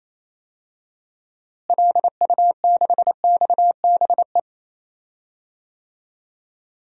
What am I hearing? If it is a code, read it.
Morse code, heard as a keyed tone, about 28 wpm, 700 Hz. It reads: LU6XBE
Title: DXCC-100 Call Signs - 6 Characters - Repeated 2x - 28wpm